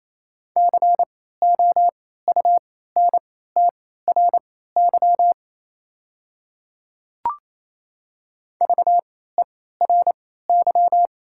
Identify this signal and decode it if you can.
Morse code, heard as a keyed tone, about 27 wpm, spaced as for 16 wpm, 690 Hz. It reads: COUNTRY VERY
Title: 3x Character Spacing - Top 300 Words - 28wpm